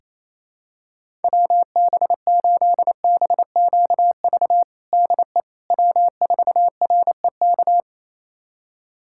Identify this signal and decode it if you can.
Morse code, heard as a keyed tone, about 28 wpm, 700 Hz. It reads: WB8BQV DE W4REK
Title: USA Callsign DE Callsign 28wpm